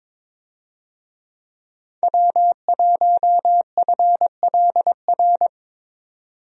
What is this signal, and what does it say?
Morse code, heard as a keyed tone, about 22 wpm, 700 Hz. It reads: W1FLR
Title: USA Call Signs - 5 Characters - 22wpm